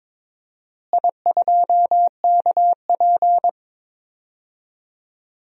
Morse code, heard as a keyed tone, about 22 wpm, 700 Hz. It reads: I2KP
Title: DXCC-100 Call Signs - 4 Characters - Repeated 3x - 22wpm